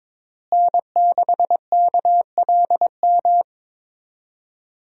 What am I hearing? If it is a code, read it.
Morse code, heard as a keyed tone, about 22 wpm, 700 Hz. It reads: N6KLM
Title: USA Call Signs - 5 Characters - Repeated 2x - 22wpm